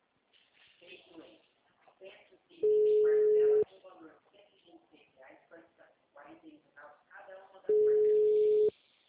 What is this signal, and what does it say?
Ringback tone (Europe, 425 Hz), with speech.